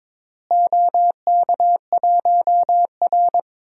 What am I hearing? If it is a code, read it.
Morse code, heard as a keyed tone, about 21 wpm, 700 Hz. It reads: OK1R
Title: DXCC-100 Call Signs - 4 Characters - Repeated 2x - 22wpm